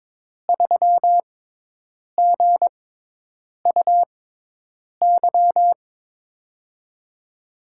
Morse code, heard as a keyed tone, about 22 wpm, spaced as for 8 wpm, 700 Hz. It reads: 3GUY
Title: USA Call Signs - 5 Characters - 6x Character Spacing - 22wpm